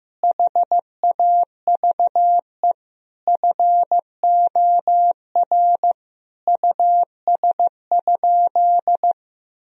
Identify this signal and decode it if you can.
Morse code, heard as a keyed tone, about 15 wpm, 700 Hz. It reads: HAVE FOR US?
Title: Sentences from Top 1000 Words - Review Part 3 - 15wpm